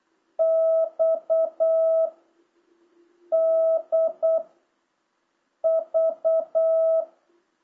Morse code, heard as a keyed tone, about 8 wpm, 650 Hz. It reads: X D V